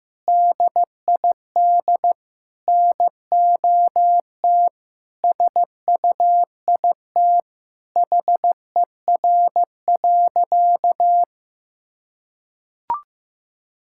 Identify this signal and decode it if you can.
Morse code, heard as a keyed tone, about 15 wpm, 700 Hz. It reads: DID NOT SUIT HER.
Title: Sentences from Top 1600 Words 15wpm